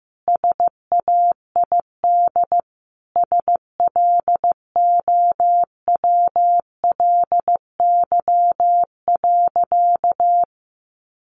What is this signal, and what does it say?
Morse code, heard as a keyed tone, about 15 wpm, 700 Hz. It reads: SAID SLOWLY.
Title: Sentences from Top 1100 Words 15wpm